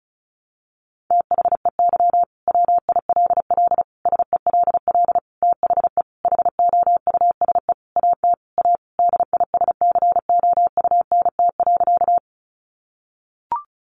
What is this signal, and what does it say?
Morse code, heard as a keyed tone, about 34 wpm, 700 Hz. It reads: THEY WILL SELL THE HOUSE AT A DISCOUNT.
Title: Sentences from Top 1200 Words 35wpm